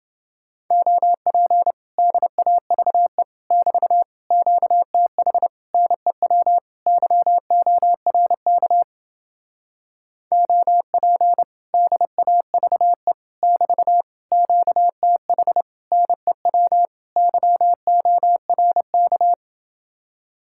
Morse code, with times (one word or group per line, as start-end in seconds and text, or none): 0.7-1.7 OP
2.0-3.2 DAVE
3.5-4.0 <BT>
4.3-5.5 QTH
5.7-6.6 NEW
6.9-8.8 YORK
10.3-11.4 OP
11.7-13.1 DAVE
13.4-14.0 <BT>
14.3-15.6 QTH
15.9-16.9 NEW
17.2-19.3 YORK